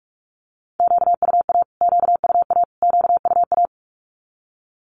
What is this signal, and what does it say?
Morse code, heard as a keyed tone, about 44 wpm, 700 Hz. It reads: QUA QUA QUA
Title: Q-Codes Repeated 3x 45wpm